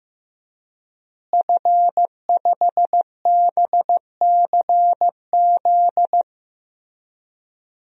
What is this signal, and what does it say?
Morse code, heard as a keyed tone, about 15 wpm, 700 Hz. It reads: F5BCZ
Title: DXCC-100 Call Signs - 5 Characters - Repeated 3x - 15wpm